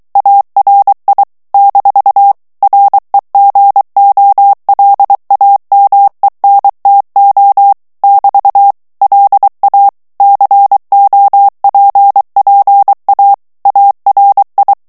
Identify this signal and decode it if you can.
Morse code, heard as a keyed tone, about 23 wpm, 800 Hz. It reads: ARI - REGOLAMENTO <BT> LA COPPA ARI
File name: ITALIEN texte clair 2 vitesse 22 mots minute